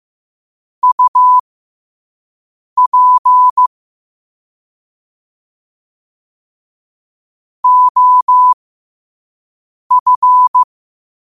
Morse code, heard as a keyed tone, about 15 wpm, spaced as for 5 wpm, 1000 Hz. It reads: UP OF
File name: qotdfiles / 5wpm